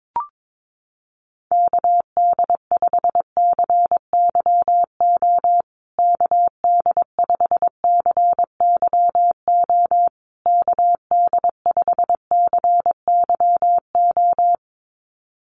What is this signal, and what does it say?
Morse code, heard as a keyed tone, about 22 wpm, 700 Hz. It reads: KD5CYO KD5CYO KD5CYO
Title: USA Call Signs Repeated 3x 22wpm